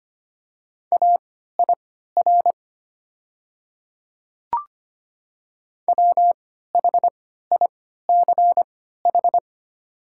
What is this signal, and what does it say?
Morse code, heard as a keyed tone, about 25 wpm, spaced as for 14 wpm, 690 Hz. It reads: AIR WHICH
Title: 3x Character Spacing - Top 200 Words - 25wpm